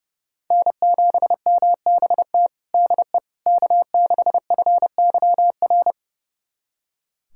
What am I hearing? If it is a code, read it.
Morse code, heard as a keyed tone, about 31 wpm, 700 Hz. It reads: N7MBT DE K6FYR
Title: USA Callsign DE Callsign - 5 Characters - 30wpm